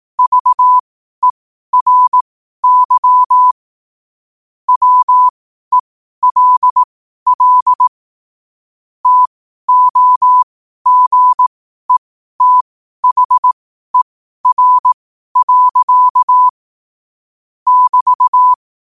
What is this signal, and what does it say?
Morse code, heard as a keyed tone, about 18 wpm, spaced as for 12 wpm, 1000 Hz. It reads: VERY WELL TOGETHER. <BT>